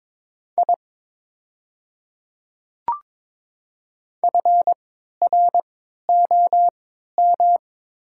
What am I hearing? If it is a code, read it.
Morse code, heard as a keyed tone, about 22 wpm, spaced as for 13 wpm, 700 Hz. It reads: I FROM
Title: 3x Character Spacing - Top 300 Words - 22wpm